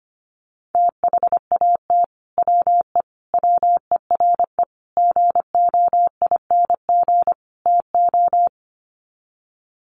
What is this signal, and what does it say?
Morse code, heard as a keyed tone, about 25 wpm, 700 Hz. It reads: THAT WE WERE GOING TO
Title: Sets of 5 Words Encore2 25wpm